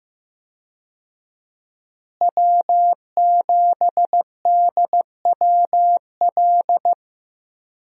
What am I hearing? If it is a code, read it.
Morse code, heard as a keyed tone, about 15 wpm, 700 Hz. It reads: W7DWL
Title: USA Call Signs - 5 Characters - Repeated 2x - 15wpm